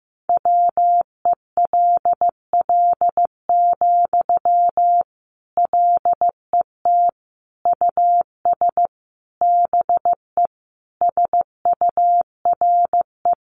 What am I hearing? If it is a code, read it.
Morse code, heard as a keyed tone, about 15 wpm, 700 Hz. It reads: WELL, LET US BE SURE
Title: Sentences from Top 300 Words 15wpm